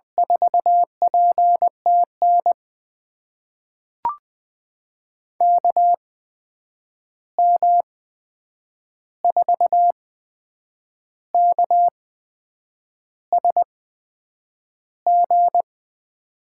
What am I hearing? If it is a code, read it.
Morse code, heard as a keyed tone, about 21 wpm, spaced as for 8 wpm, 700 Hz. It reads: 4PTN KM4KSG
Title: USA Call Signs - 8x Character Spacing - 20wpm